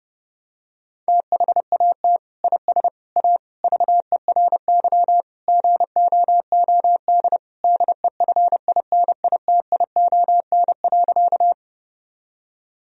Morse code, heard as a keyed tone, about 30 wpm, 700 Hz. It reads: THAT IS A VERY GOOD DEFINITION.
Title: Sentences from Top 1000 Words - Review Part 4 - 30wpm